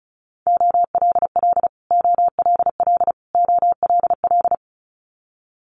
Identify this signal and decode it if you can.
Morse code, heard as a keyed tone, about 36 wpm, 700 Hz. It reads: OLL OLL OLL
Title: Multi-letter Phonemes Repeated 3x 35wpm